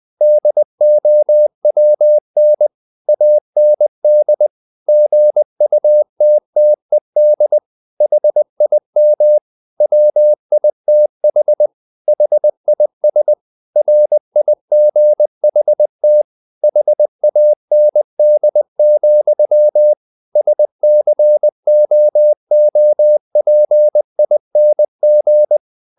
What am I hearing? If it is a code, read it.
Morse code, heard as a keyed tone, about 20 wpm, 600 Hz. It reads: DOWN AND GUTTED HIM WITH HIS RIGHT HAND, SCOOPING
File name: the old man and the sea part 2 20wpm